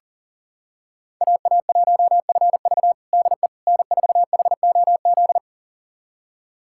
Morse code, heard as a keyed tone, about 40 wpm, 700 Hz. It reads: AA1FV DE N4HOZ
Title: USA Callsign DE Callsign - 5 Characters - 40wpm